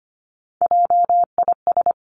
Morse code, heard as a keyed tone, about 25 wpm, 700 Hz. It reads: JIS
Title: Three Challenging Letter-Number Combinations 25wpm